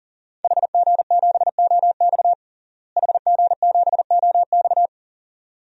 Morse code, heard as a keyed tone, about 40 wpm, 700 Hz. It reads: HG7OX HG7OX
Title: DXCC-100 Call Signs - 5 Characters - Repeated 2x - 40wpm